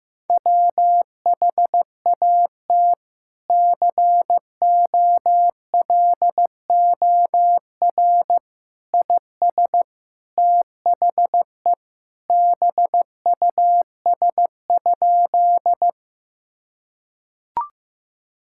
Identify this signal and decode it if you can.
Morse code, heard as a keyed tone, about 15 wpm, 700 Hz. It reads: WHAT COLOR IS THE BUS?
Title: Sentences from Top 1000 Words - Review Part 6 - 15wpm